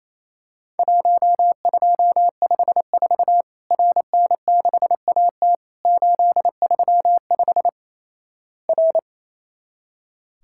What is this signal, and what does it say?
Morse code, heard as a keyed tone, about 28 wpm, 700 Hz. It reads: 1254 RN6AT 835 R
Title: Worked All Europe DX Contest - QTC Exchange - 28wpm